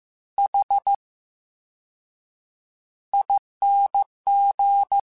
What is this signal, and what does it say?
Morse code, heard as a keyed tone, about 15 wpm, spaced as for 7 wpm, 800 Hz. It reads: HING
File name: Special INGs / Special INGs-mp3s / Slow ing @13